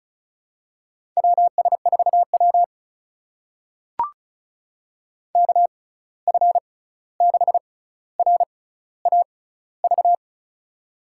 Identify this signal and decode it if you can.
Morse code, heard as a keyed tone, about 36 wpm, spaced as for 16 wpm, 700 Hz. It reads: WS4W KF6RAV
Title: USA Call Signs - 6x Character Spacing - 35wpm